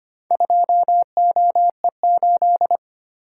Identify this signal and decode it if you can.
Morse code, heard as a keyed tone, about 25 wpm, 690 Hz. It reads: 2OE8
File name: practice / 25wpm / Four Letter-Number Combinations - 25wpm